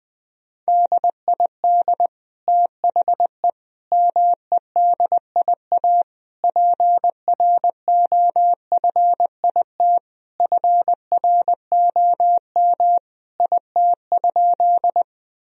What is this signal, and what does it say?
Morse code, heard as a keyed tone, about 21 wpm, 700 Hz. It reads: DID THE MEDIA PROFIT FROM IT?
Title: Sentences from Top 1000 Words - Review Part 4 - 20wpm